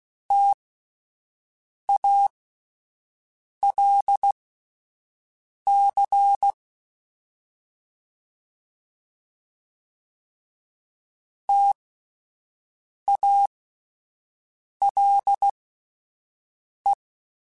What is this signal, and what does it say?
Morse code, heard as a keyed tone, about 16 wpm, spaced as for 5 wpm, 800 Hz. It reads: TALC TALE